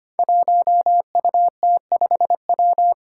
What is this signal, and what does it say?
Morse code, heard as a keyed tone, about 25 wpm, 700 Hz. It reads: 1UT5W
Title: Five Challenging Letter-Number Combinations 25wpm